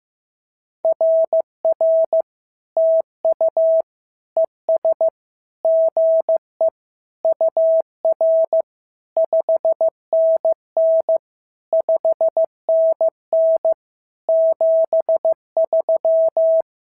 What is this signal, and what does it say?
Morse code, heard as a keyed tone, about 15 wpm, 650 Hz. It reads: RR TU ES GE UR 5NN 5NN 73
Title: SOTA QSO Practice 15wpm